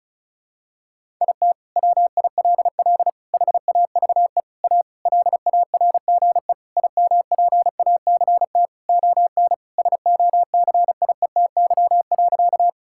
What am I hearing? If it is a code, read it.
Morse code, heard as a keyed tone, about 34 wpm, 700 Hz. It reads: IT WILL HAVE A LARGE IMPACT ON SOCIETY.
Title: Sentences from Top 1000 Words - Review Part 6 - 35wpm